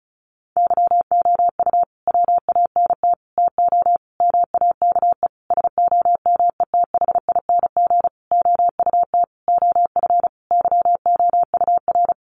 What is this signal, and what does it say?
Morse code, heard as a keyed tone, about 34 wpm, 700 Hz. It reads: YOU WANT TO MAKE SOMETHING OUT OF YOUR
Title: Sentences from Top 600 Words 35wpm